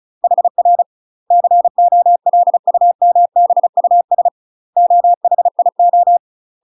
Morse code, heard as a keyed tone, about 34 wpm, 700 Hz. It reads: HR COLUMBUS OHIO
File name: UK MASTER QSOs TRAINING / 35WPM / UK MASTER QSOs 0038